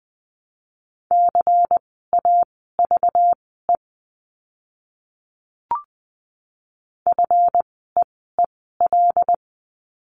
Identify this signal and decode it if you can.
Morse code, heard as a keyed tone, about 20 wpm, spaced as for 15 wpm, 700 Hz. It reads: CAVE FEEL
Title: 192 Lesson 31f - Letter v - Words 20wpm